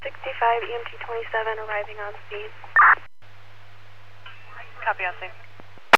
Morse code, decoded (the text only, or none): none